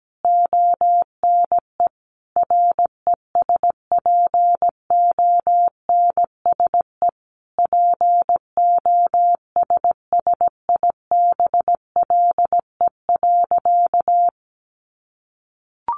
Morse code, heard as a keyed tone, about 17 wpm, 700 Hz. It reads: ONE RESPONSE POSSIBLE.
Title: Sentences from Top 1400 Words 17wpm